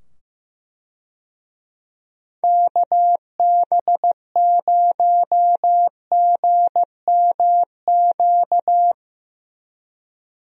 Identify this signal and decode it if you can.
Morse code, heard as a keyed tone, about 15 wpm, 700 Hz. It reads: KB0GMQ
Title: USA Call Signs - 6 Characters - 15wpm